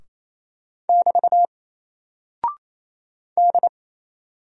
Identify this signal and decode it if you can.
Morse code, heard as a keyed tone, about 27 wpm, 700 Hz. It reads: <BT> D
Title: Single Character (Letter:Number:Punctuation:Prosign) 28wpm